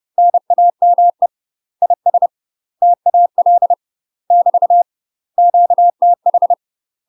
Morse code, heard as a keyed tone, about 30 wpm, 700 Hz. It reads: NAME IS TAL <BT> QTH
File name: UK MASTER QSOs TRAINING / 30WPM / UK MASTER QSOs 0046